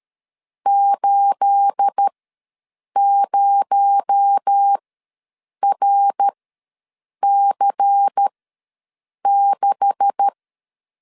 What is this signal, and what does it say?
Morse code, heard as a keyed tone, about 13 wpm, spaced as for 7 wpm, 800 Hz. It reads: 80RC6
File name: LEZIONI / 9 286 ws 12wpm 3351